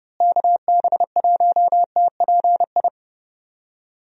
Morse code, heard as a keyed tone, about 30 wpm, 700 Hz. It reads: KB1TPI
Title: USA Call Signs - 6 Characters - Repeated 3x - 30wpm